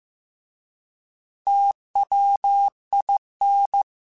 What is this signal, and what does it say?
Morse code, heard as a keyed tone, about 15 wpm, 800 Hz. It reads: TWIN